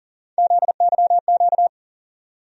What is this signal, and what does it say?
Morse code, heard as a keyed tone, about 40 wpm, 700 Hz. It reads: ZYQ
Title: USA Call Sign Postfixes - 3 Letters 40wpm